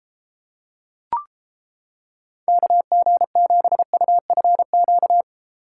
Morse code, heard as a keyed tone, about 30 wpm, 700 Hz. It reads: KG7UFQ